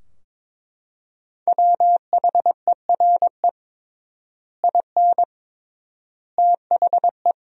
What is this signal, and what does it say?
Morse code, heard as a keyed tone, about 22 wpm, 700 Hz. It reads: WHERE IN THE
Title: 3x Word-Spacing - Sentences from Top 200 Words - 22wpm